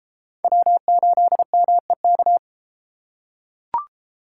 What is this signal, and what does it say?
Morse code, heard as a keyed tone, about 29 wpm, 700 Hz. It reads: W8MEK